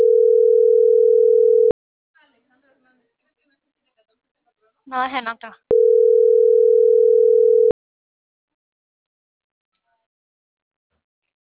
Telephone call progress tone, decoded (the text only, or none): ringback tone